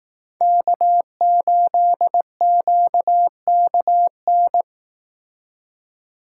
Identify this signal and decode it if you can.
Morse code, heard as a keyed tone, about 19 wpm, 700 Hz. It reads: K8QKN